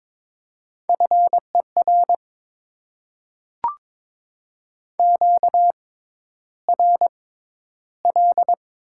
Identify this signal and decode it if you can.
Morse code, heard as a keyed tone, about 22 wpm, spaced as for 11 wpm, 690 Hz. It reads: FER QRL
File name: Top 20 QSO Elements - 6x Character Spacing - 22wpm